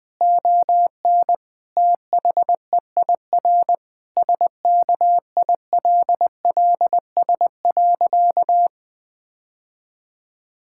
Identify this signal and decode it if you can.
Morse code, heard as a keyed tone, about 20 wpm, 700 Hz. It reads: ON THEIR SKILLS.
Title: Sentences with 2-Word Phrases - 20wpm